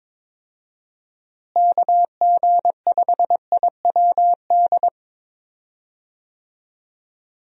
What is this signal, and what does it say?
Morse code, heard as a keyed tone, about 22 wpm, 700 Hz. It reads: KG5IWD
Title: USA Call Signs - 6 Characters - Repeated 2x - 22wpm